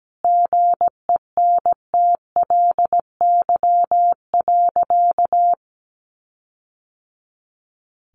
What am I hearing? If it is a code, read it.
Morse code, heard as a keyed tone, about 17 wpm, 700 Hz. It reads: GENTLY.
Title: Sentences from Top 1500 Words 17wpm